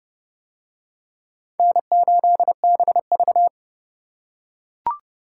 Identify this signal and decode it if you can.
Morse code, heard as a keyed tone, about 28 wpm, 700 Hz. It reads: N8BV